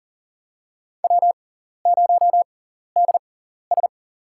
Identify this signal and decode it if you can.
Morse code, heard as a keyed tone, about 37 wpm, spaced as for 14 wpm, 700 Hz. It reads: W0DS
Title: USA Call Signs - 6x Character Spacing - 40wpm